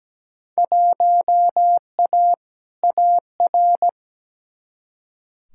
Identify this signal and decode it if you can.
Morse code, heard as a keyed tone, about 17 wpm, 700 Hz. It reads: 1A AR
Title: Field Day Exchanges Encore - 17wpm